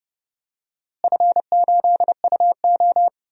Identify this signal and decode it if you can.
Morse code, heard as a keyed tone, about 30 wpm, 700 Hz. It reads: F8UO
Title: DXCC-100 Call Signs - 4 Characters - Repeated 2x - 30wpm